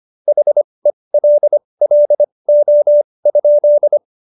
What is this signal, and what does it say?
Morse code, heard as a keyed tone, about 25 wpm, spaced as for 20 wpm, 590 Hz. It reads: HELLO?
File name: Jokes-25-char-20-eff